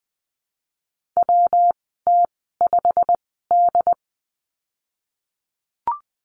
Morse code, heard as a keyed tone, about 20 wpm, spaced as for 15 wpm, 700 Hz. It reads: WT5D
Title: 173 Lesson 28g - Number 9 - Call Signs 20wpm